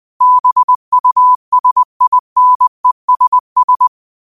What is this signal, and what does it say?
Morse code, heard as a keyed tone, about 20 wpm, 1000 Hz. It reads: BUSINESS